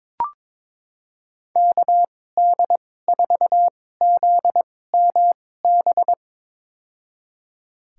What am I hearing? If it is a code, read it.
Morse code, heard as a keyed tone, about 22 wpm, spaced as for 16 wpm, 700 Hz. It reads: KD4ZMB
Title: USA Call Signs - 6 Characters - 2x Character Spacing - 22wpm